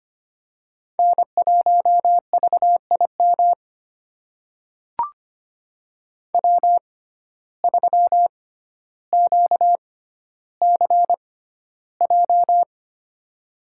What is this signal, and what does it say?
Morse code, heard as a keyed tone, about 25 wpm, spaced as for 13 wpm, 700 Hz. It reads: N1VIM W3QCJ
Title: USA Call Signs - 5 Characters - 6x Character Spacing - 25wpm